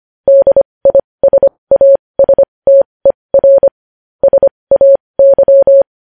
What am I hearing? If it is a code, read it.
Morse code, heard as a keyed tone, about 25 wpm, spaced as for 20 wpm, 550 Hz. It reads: DISASTER SAY
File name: podcast / News headlines-1x-25x20